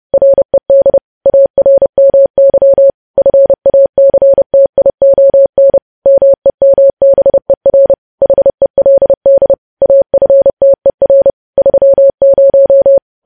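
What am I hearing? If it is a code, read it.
Morse code, heard as a keyed tone, about 30 wpm, 550 Hz. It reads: RED ARMY FACTION MEMBER HELD AFTER 30
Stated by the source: Morse code news headlines sent at 30, preceded by vvv vvv.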